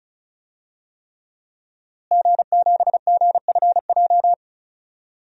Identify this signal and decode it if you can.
Morse code, heard as a keyed tone, about 36 wpm, 700 Hz. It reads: G7GFJ
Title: DXCC-100 Call Signs - 5 Characters - Repeated 3x - 35wpm